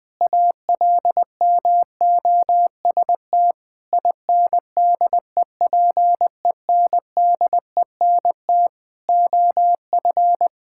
Morse code, heard as a keyed tone, about 20 wpm, 700 Hz. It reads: ALMOST INDEPENDENT OF
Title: Sentences from Top 1000 Words - Review Part 6 - 20wpm